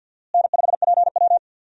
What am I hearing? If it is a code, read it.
Morse code, heard as a keyed tone, about 52 wpm, 700 Hz. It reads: N5PW
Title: USA Call Signs - 4 Characters - Repeated 2x - 50wpm